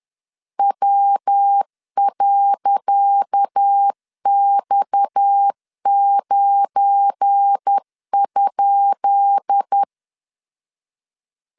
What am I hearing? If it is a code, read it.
Morse code, heard as a keyed tone, about 10 wpm, 800 Hz. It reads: W.X9?
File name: LEZIONI / 10 SIGNS / 10C DOT 10wpm 2940